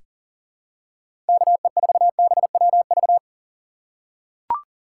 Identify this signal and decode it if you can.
Morse code, heard as a keyed tone, about 37 wpm, 700 Hz. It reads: KE4BWV
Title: USA Call Signs - 6 Characters - 4x Character Spacing - 40wpm